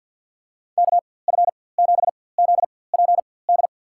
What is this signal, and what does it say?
Morse code, heard as a keyed tone, about 51 wpm, spaced as for 23 wpm, 690 Hz. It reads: KF7ZPD
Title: USA Call Signs - 4x Character Spacing - 50wpm